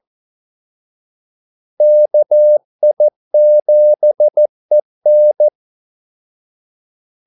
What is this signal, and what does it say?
Morse code, heard as a keyed tone, about 14 wpm, 600 Hz. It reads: KI7EN
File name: Callsigns-14wpm